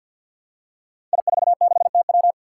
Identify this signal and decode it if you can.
Morse code, heard as a keyed tone, about 53 wpm, 700 Hz. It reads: I46TW
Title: Five Challenging Letter-Number Combinations 50wpm